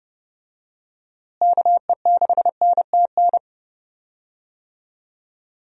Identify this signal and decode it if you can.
Morse code, heard as a keyed tone, about 30 wpm, 700 Hz. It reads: KE6NTN
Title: USA Call Signs - 6 Characters - Repeated 3x - 30wpm